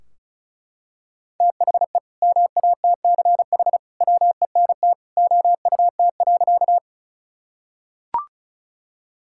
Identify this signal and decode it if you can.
Morse code, heard as a keyed tone, about 34 wpm, 700 Hz. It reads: THE MATCH WENT OUT.
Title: Sentences from Top 1600 Words 35wpm